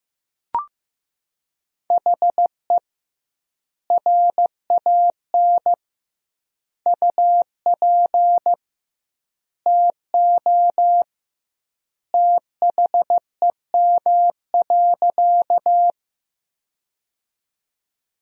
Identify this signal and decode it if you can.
Morse code, heard as a keyed tone, about 15 wpm, 700 Hz. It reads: HE RAN UP TO THEM.
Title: 2x Word-Spacing - Sentences from Top 500 Words - 15wpm